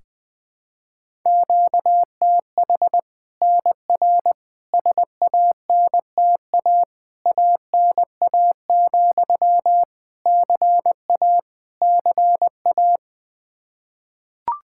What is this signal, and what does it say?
Morse code, heard as a keyed tone, about 20 wpm, 700 Hz. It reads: QTH NR SANTA ANA, CA CA
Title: Practice QSO - Exchange 1 - Beginning - 20wpm